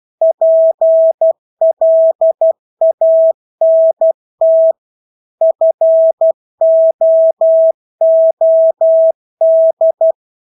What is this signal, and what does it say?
Morse code, heard as a keyed tone, about 12 wpm, 650 Hz. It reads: PLANT FOOD
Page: Engelske ord 12 wpm